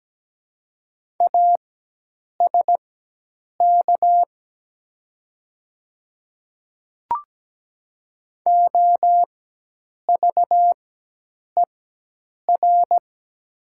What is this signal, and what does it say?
Morse code, heard as a keyed tone, about 17 wpm, spaced as for 8 wpm, 700 Hz. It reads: ASK OVER
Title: 4x Character Spacing - Top 300 Words - 17wpm